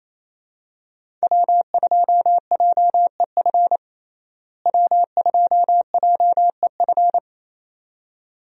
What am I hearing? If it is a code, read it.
Morse code, heard as a keyed tone, about 28 wpm, 700 Hz. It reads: W2JEF W2JEF
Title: USA Call Signs - 5 Characters - Repeated 2x - 28wpm